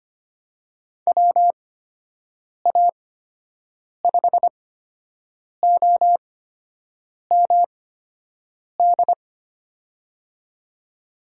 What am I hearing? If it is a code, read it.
Morse code, heard as a keyed tone, about 25 wpm, spaced as for 7 wpm, 700 Hz. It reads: WA5OMD
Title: USA Call Signs - 6 Characters - 8x Character Spacing - 25wpm